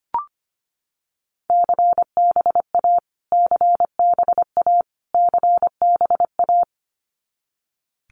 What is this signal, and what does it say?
Morse code, heard as a keyed tone, about 25 wpm, 700 Hz. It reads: CBA CBA CBA
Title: Abbreviations Repeated 3x - 25wpm